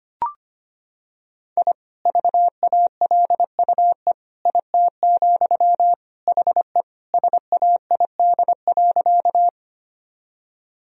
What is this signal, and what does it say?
Morse code, heard as a keyed tone, about 25 wpm, 700 Hz. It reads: I VALUE IT, HE SAID.
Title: Sentences from Top 1000 Words - Review Part 3 - 25wpm